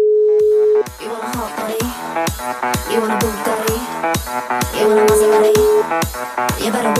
Ringback tone (Europe, 425 Hz), with noise.